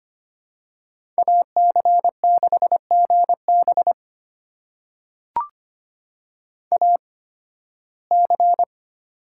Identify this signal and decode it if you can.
Morse code, heard as a keyed tone, about 25 wpm, spaced as for 16 wpm, 700 Hz. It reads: AC6GB AC
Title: USA Call Signs - 5 Characters - 8x Character Spacing - 25wpm